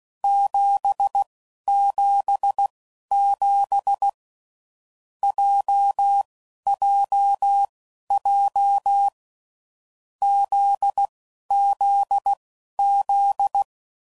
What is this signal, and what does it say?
Morse code, heard as a keyed tone, about 16 wpm, spaced as for 12 wpm, 800 Hz. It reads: 777 JJJ ZZZ
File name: Groups / Basic Letter Groups / JZ17